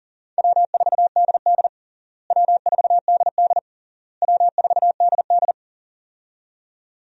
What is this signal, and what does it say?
Morse code, heard as a keyed tone, about 40 wpm, 700 Hz. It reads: W4DD W4DD W4DD
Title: USA Call Signs - 4 Characters - Repeated 3x - 40wpm